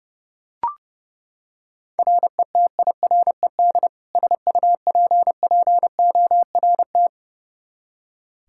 Morse code, heard as a keyed tone, about 30 wpm, 700 Hz. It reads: RETIRED SUPPORT
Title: K1USN SST QSO Practice 30wpm